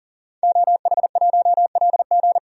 Morse code, heard as a keyed tone, about 40 wpm, 700 Hz. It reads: OH1LG
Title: DXCC-100 Call Signs - 5 Characters - 40wpm